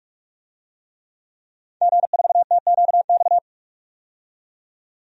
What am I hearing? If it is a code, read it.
Morse code, heard as a keyed tone, about 44 wpm, 690 Hz. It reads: G4TQX
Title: DXCC-100 Call Signs - 5 Characters - 45wpm